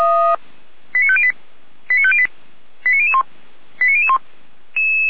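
Five-tone selective call (ZVEI), heard as quick runs of five tones, tones of about 70 ms. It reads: 89508 89508 89001 89001
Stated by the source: ZVEI five-tone call received by SDR.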